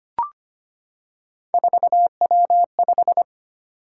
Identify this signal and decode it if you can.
Morse code, heard as a keyed tone, about 25 wpm, 700 Hz. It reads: 4W5
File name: Three Letter-Number Combinations - 25wpm